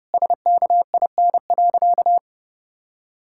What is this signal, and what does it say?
Morse code, heard as a keyed tone, about 28 wpm, 700 Hz. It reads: SKIN.
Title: Sentences from Top 1600 Words 30wpm